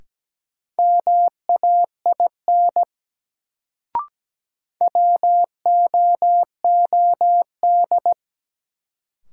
Morse code, heard as a keyed tone, about 17 wpm, 700 Hz. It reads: MAIN WOOD
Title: Sentences from Top 1000 Words - Review Part 2 - New Word Drill - 17wpm